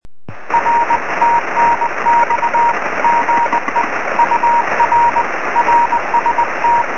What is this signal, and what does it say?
Morse code, heard as a keyed tone, about 19 wpm, 940 Hz, with radio noise. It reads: R TNX GE UR RST
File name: sm7lcb / 20 23cm / 2051 df1hf 23cm